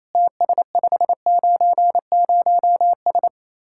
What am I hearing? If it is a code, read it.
Morse code, heard as a keyed tone, about 34 wpm, 700 Hz. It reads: TS590S
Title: Speed-Racing Common QSO Phrases Encore - 28wpm